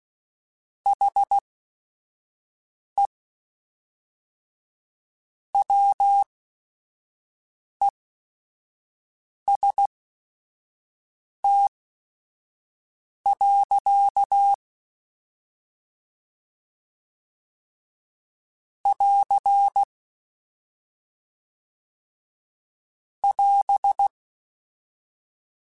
Morse code, heard as a keyed tone, about 16 wpm, spaced as for 5 wpm, 800 Hz. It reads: HE WEST. <AR> <AS>
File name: Weather Sets / Weather Sets-mp3s / Weather 5 1 @5